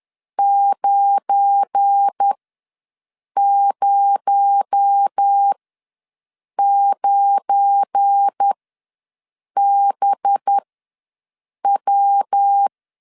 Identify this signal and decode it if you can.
Morse code, heard as a keyed tone, about 10 wpm, spaced as for 6 wpm, 800 Hz. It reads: 909BW
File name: LEZIONI / 7 GF9Z / 7 GF9 ws 10wpm 4034